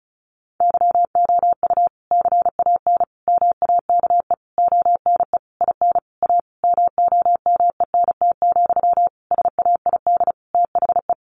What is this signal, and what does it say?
Morse code, heard as a keyed tone, about 34 wpm, 700 Hz. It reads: YOU CAN MAKE ONE IN A MOMENT, SAID THE
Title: Sentences from Top 1000 Words 35wpm